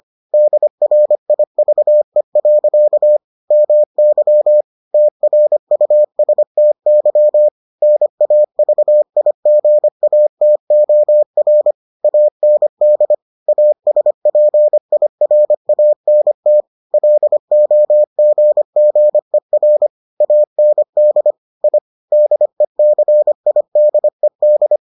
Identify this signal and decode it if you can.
Morse code, heard as a keyed tone, about 25 wpm, 600 Hz. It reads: DRIVE. MY TRUSTY NAVIGATOR AND ASPIRANT LOGGER AND I DECIDED